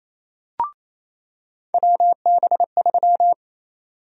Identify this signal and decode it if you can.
Morse code, heard as a keyed tone, about 28 wpm, 700 Hz. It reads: WB3
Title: USA Call Sign Prefixes - 28wpm